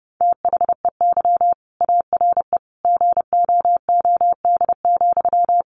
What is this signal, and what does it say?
Morse code, heard as a keyed tone, about 30 wpm, 700 Hz. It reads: THEY ARE GOOD,
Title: Sentences from Top 1100 Words 30wpm